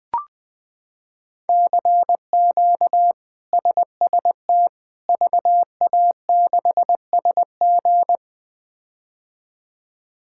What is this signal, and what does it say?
Morse code, heard as a keyed tone, about 20 wpm, 700 Hz. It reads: CQ SST VA6SG
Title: K1USN SST QSO Practice - 20wpm